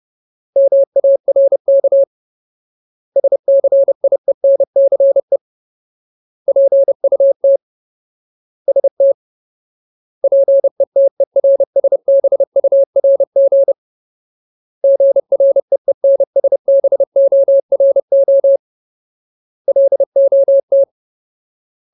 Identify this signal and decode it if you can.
Morse code, heard as a keyed tone, about 30 wpm, 550 Hz. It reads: MARK SCIENCE PUT ST PETERSBURG GREENSBORO LOT
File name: Ultimate Word List / 30WPM Ultimate